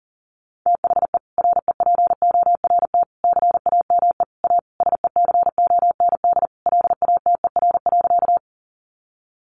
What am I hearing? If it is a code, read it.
Morse code, heard as a keyed tone, about 40 wpm, 700 Hz. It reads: THE REPORT CAME A SECOND LATER.
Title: Sentences from Top 900 Words 40wpm